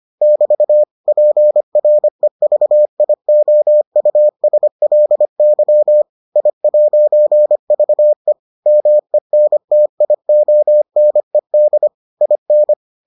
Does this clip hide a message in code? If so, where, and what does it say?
Yes: Morse code from 0.2 to 12.7 s, heard as a keyed tone, about 25 wpm, 600 Hz. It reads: <BT> PREVIOUSLY I'VE MENTIONED IN